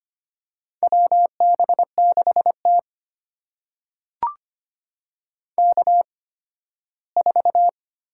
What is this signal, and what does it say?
Morse code, heard as a keyed tone, about 25 wpm, spaced as for 15 wpm, 700 Hz. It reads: WB6T K4
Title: USA Call Signs - 4 Characters - 8x Character Spacing - 25wpm